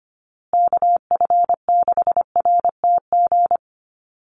Morse code, heard as a keyed tone, about 25 wpm, 700 Hz. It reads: KF6RTG
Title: USA Call Signs Repeated 3x 25wpm